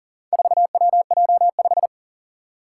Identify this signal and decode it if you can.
Morse code, heard as a keyed tone, about 40 wpm, 700 Hz. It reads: 4WJ5
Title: Four Challenging Letter-Number Combinations 40wpm